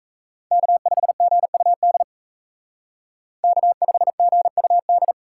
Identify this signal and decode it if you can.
Morse code, heard as a keyed tone, about 35 wpm, 700 Hz. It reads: K5GUD K5GUD